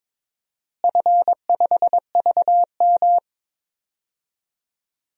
Morse code, heard as a keyed tone, about 22 wpm, 700 Hz. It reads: F5VM
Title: DXCC-100 Call Signs - 4 Characters - Repeated 3x - 22wpm